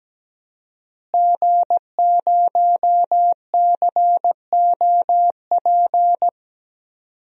Morse code, heard as a keyed tone, about 17 wpm, 700 Hz. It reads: G0COP
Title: DXCC-100 Call Signs - 5 Characters - Repeated 3x - 17wpm